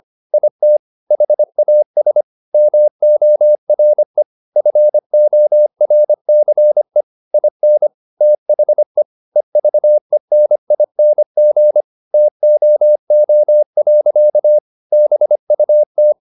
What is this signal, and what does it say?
Morse code, heard as a keyed tone, about 25 wpm, 600 Hz. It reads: IT HAS MORE FORCE IN THE EVENING TOO. BUT
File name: the old man and the sea part 1 25wpm